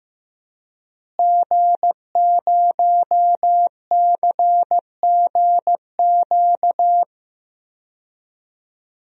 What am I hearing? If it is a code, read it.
Morse code, heard as a keyed tone, about 15 wpm, 700 Hz. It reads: G0CGQ
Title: DXCC-100 Call Signs - 5 Characters - Repeated 3x - 15wpm